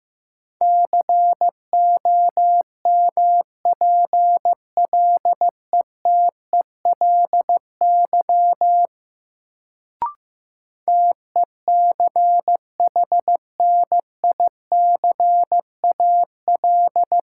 Morse code, heard as a keyed tone, about 15 wpm, 700 Hz. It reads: COMPLETELY TECHNICAL
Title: Sentences from Top 1100 Words - New Word Drill - 15wpm